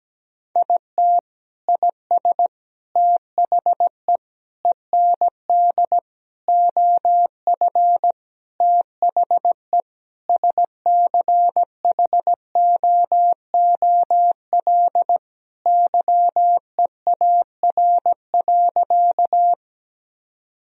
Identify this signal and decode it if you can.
Morse code, heard as a keyed tone, about 17 wpm, 700 Hz. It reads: IT IS THE END OF THE SCHOOL YEAR.
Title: Sentences from Top 1000 Words - Review Part 1 - 17wpm